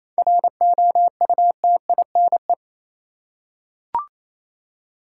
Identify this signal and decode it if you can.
Morse code, heard as a keyed tone, about 28 wpm, 700 Hz. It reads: ROUTINE
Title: Seven Letter Words - 28wpm